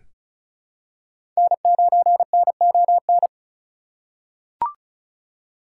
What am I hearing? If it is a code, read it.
Morse code, heard as a keyed tone, about 36 wpm, 700 Hz. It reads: N9NON
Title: USA Call Signs - 5 Characters - 3x Character Spacing - 35wpm